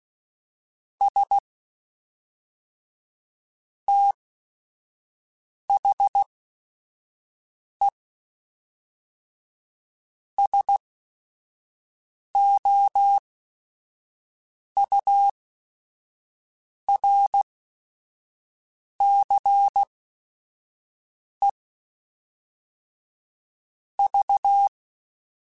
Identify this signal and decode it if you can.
Morse code, heard as a keyed tone, about 16 wpm, spaced as for 5 wpm, 800 Hz. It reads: S THE SOURCE V
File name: Technical Subjects / Technical Subjects-mp3s / XL @5